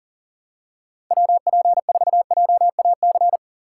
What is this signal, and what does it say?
Morse code, heard as a keyed tone, about 38 wpm, 700 Hz. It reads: WP4JAC
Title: USA Call Signs - 6 Characters - 8x Character Spacing - 40wpm